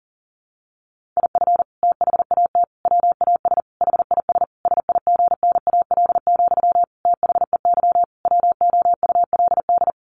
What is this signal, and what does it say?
Morse code, heard as a keyed tone, about 40 wpm, 700 Hz. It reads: IF THAT WAS HIS SIGNAL, THEY WOULD
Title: Sentences from Top 1400 Words 40wpm